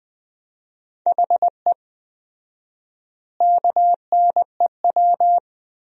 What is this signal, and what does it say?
Morse code, heard as a keyed tone, about 20 wpm, 700 Hz. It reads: HE KNEW
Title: 4x Word-Spacing - Sentences from Top 400 Words - 20wpm